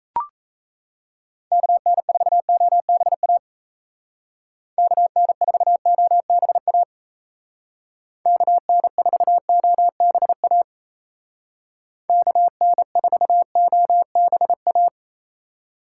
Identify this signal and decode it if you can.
Morse code, heard as a keyed tone, about 33 wpm, 700 Hz. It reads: KN4OBA KN4OBA KN4OBA KN4OBA
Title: Speed Racing - USA 2x3 Call Signs 28wpm